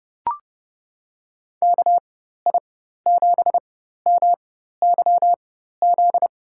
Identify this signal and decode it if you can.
Morse code, heard as a keyed tone, about 30 wpm, spaced as for 14 wpm, 700 Hz. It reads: KI7MYZ
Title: USA Call Signs - 6 Characters - 4x Character Spacing - 30wpm